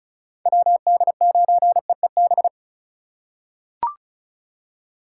Morse code, heard as a keyed tone, about 36 wpm, 700 Hz. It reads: WD9EEB
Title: USA Call Signs - 6 Characters - 3x Character Spacing - 35wpm